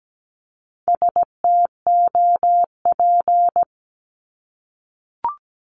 Morse code, heard as a keyed tone, about 17 wpm, 700 Hz. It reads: STOP
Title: Multi-letter Phonemes 17wpm